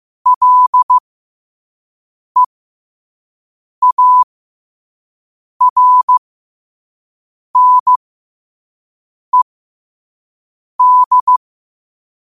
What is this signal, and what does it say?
Morse code, heard as a keyed tone, about 15 wpm, spaced as for 5 wpm, 1000 Hz. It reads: LEARNED